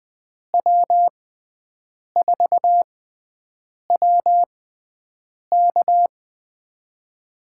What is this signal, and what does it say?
Morse code, heard as a keyed tone, about 20 wpm, spaced as for 7 wpm, 700 Hz. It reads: W4WK
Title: USA Call Signs - 5 Characters - 6x Character Spacing - 20wpm